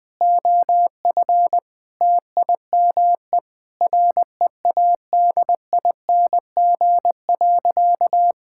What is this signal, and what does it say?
Morse code, heard as a keyed tone, about 20 wpm, 700 Hz. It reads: OF TIME READING.
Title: Sentences with 5-Word Phrases - Spoken Beginning - 20wpm